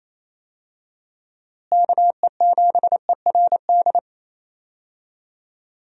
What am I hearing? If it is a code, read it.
Morse code, heard as a keyed tone, about 27 wpm, 700 Hz. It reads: KE7ERD
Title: USA Call Signs - 6 Characters - Repeated 2x - 28wpm